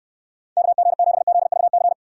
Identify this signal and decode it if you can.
Morse code, heard as a keyed tone, about 66 wpm, 700 Hz. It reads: KG7ZVC